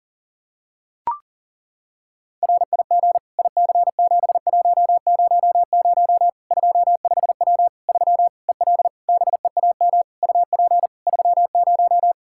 Morse code, heard as a keyed tone, about 40 wpm, 700 Hz. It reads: RIG IC7100 25W 3 EL BEAM UP 30
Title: Practice QSO - Exchange 2 - Info - 40wpm